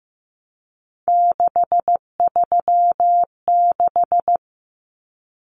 Morse code, heard as a keyed tone, about 15 wpm, 700 Hz. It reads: 636
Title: Numbers - 3-Digit 15wpm